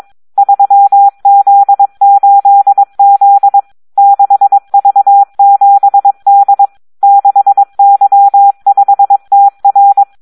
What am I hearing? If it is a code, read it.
Morse code, heard as a keyed tone, about 22 wpm, 800 Hz. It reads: 3Z8Z 6V7D 6Y5TR
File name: Audio practica CW / SD 1 22